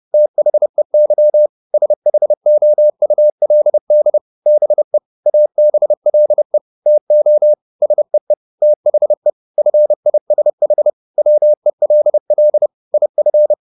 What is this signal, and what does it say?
Morse code, heard as a keyed tone, about 30 wpm, 600 Hz. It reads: THEY SHOULD BE ABLE TO SEE THE FISH WELL IF
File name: the old man and the sea 2 30wpm